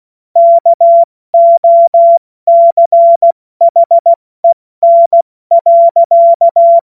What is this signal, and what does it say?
Morse code, heard as a keyed tone, about 16 wpm, 680 Hz. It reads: KOCHEN.